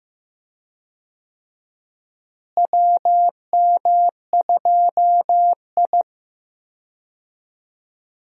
Morse code, heard as a keyed tone, about 15 wpm, 700 Hz. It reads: WM2I
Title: USA Call Signs - 4 Characters - Repeated 2x - 15wpm